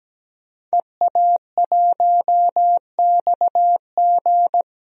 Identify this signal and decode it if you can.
Morse code, heard as a keyed tone, about 17 wpm, 700 Hz. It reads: EA1XG
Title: DXCC-100 Call Signs - 5 Characters - Repeated 2x - 17wpm